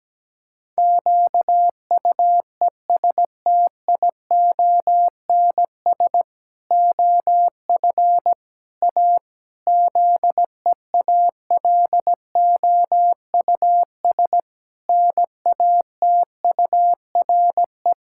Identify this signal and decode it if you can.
Morse code, heard as a keyed tone, about 17 wpm, 700 Hz. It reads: QUESTIONS OF A ZEALOUS NATURE
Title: Pangram Challenge v2 - 17wpm